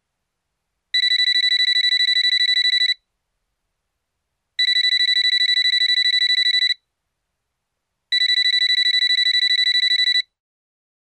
There is no code.